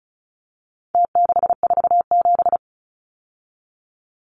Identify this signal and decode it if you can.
Morse code, heard as a keyed tone, about 34 wpm, 700 Hz. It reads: T647
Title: Numbers - Contest Style - 4-Digit 35wpm